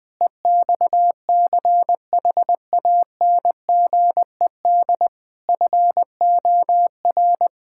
Morse code, heard as a keyed tone, about 20 wpm, 700 Hz. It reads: EXCHANGED FOR
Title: Pangram Challenge v2 - 20wpm